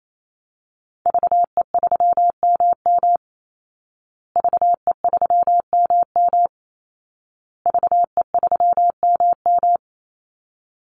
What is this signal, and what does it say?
Morse code, heard as a keyed tone, about 28 wpm, 700 Hz. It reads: VE3MM VE3MM VE3MM
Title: Call Signs Repeated 2x - 28wpm